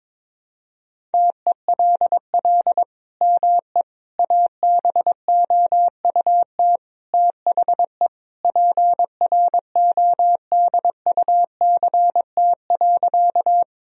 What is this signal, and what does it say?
Morse code, heard as a keyed tone, about 22 wpm, 700 Hz. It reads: TELL ME ABOUT THE PRODUCT.
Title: Sentences from Top 1000 Words - Review Part 2 - 22wpm